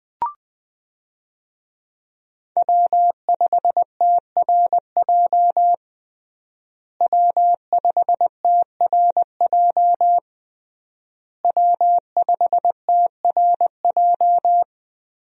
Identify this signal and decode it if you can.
Morse code, heard as a keyed tone, about 20 wpm, 700 Hz. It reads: W5TRJ W5TRJ W5TRJ
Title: USA Call Signs - 5 Characters - Repeated 3x - 20wpm